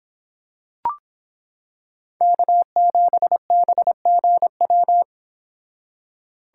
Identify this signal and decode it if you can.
Morse code, heard as a keyed tone, about 26 wpm, 700 Hz. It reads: K7BGW